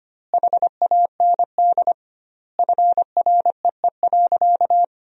Morse code, heard as a keyed tone, about 25 wpm, 700 Hz. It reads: HAND FREE.
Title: 2x Word-Spacing - Sentences from Top 500 Words - 25wpm